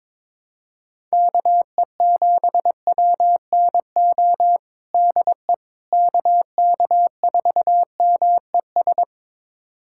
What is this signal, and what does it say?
Morse code, heard as a keyed tone, about 21 wpm, 700 Hz. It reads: KE7WNO DE KK4MES
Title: USA Callsign DE Callsign - 6 Characters - 22wpm